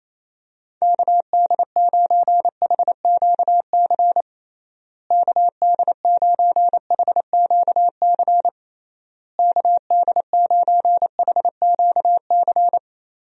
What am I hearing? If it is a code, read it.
Morse code, heard as a keyed tone, about 28 wpm, 700 Hz. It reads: KD9HQC KD9HQC KD9HQC
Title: USA Call Signs - 6 Characters - Repeated 3x - 28wpm